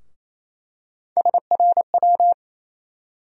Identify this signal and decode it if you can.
Morse code, heard as a keyed tone, about 28 wpm, 700 Hz. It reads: SRW
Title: USA Call Sign Postfixes - 3 Letters 28wpm